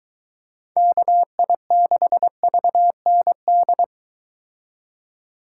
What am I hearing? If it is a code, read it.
Morse code, heard as a keyed tone, about 23 wpm, 700 Hz. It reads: KI6VND